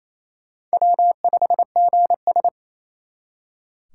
Morse code, heard as a keyed tone, about 32 wpm, 700 Hz. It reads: W5GS
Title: Speed Racing - USA Call Signs 28wpm